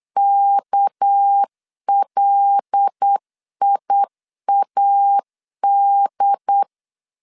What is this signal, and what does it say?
Morse code, heard as a keyed tone, about 8 wpm, 800 Hz. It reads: KLIAD
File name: LEZIONI / 6 PKX0 / 6 PKX0 8wpm 3341